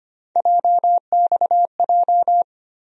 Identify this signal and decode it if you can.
Morse code, heard as a keyed tone, about 25 wpm, 700 Hz. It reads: JXJ
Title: USA Call Sign Postfixes - 3 Letters 25wpm